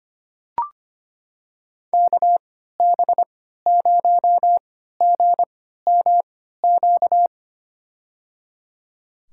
Morse code, heard as a keyed tone, about 25 wpm, spaced as for 14 wpm, 700 Hz. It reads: KB0GMQ
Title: USA Call Signs - 6 Characters - 3x Character Spacing - 25wpm